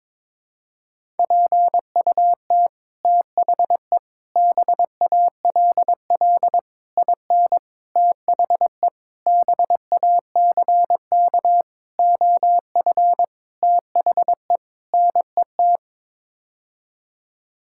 Morse code, heard as a keyed tone, about 22 wpm, 700 Hz. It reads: PUT THE BALL IN THE BACK OF THE NET
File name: Sets of 9 Words - 22wpm